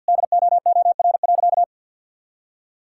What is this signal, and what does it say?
Morse code, heard as a keyed tone, about 50 wpm, 700 Hz. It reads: DOOR.
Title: Sentences from Top 1000 Words - Review Part 6 - 50wpm